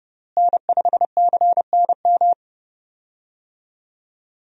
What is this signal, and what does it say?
Morse code, heard as a keyed tone, about 30 wpm, 700 Hz. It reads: N5CNM
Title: USA Call Signs - 5 Characters - Repeated 2x - 30wpm